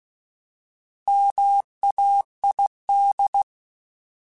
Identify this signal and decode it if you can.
Morse code, heard as a keyed tone, about 16 wpm, 800 Hz. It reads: MAID
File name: Alphabetical / Alphabetical mp3s / Ms @15